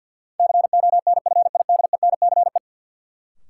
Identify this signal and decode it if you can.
Morse code, heard as a keyed tone, about 53 wpm, 700 Hz. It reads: CONFIDENCE
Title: Sentences from Top 1500 Words - New Word Drill - 50wpm